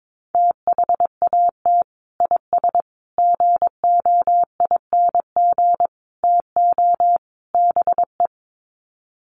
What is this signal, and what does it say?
Morse code, heard as a keyed tone, about 22 wpm, 700 Hz. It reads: THAT IS GOING TO BE
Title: Sets of 5 Words Encore2 22wpm